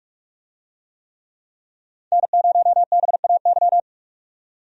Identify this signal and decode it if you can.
Morse code, heard as a keyed tone, about 46 wpm, 700 Hz. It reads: N0BAY
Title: USA Call Signs - 5 Characters - 45wpm